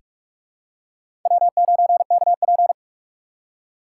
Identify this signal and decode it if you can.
Morse code, heard as a keyed tone, about 47 wpm, 700 Hz. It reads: W9KP
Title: USA Call Signs - 4 Characters - 2x Character Spacing - 45wpm